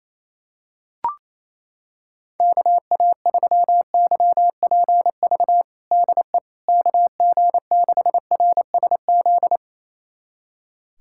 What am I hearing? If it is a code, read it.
Morse code, heard as a keyed tone, about 28 wpm, 700 Hz. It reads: KA3YPV DE KG6RSZ
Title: USA Callsign DE Callsign 28wpm